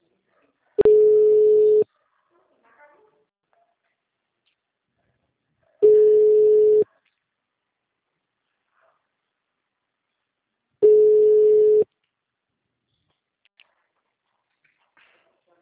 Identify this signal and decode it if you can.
Ringback tone (Europe, 425 Hz), with speech.
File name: out- -2029- - - .30057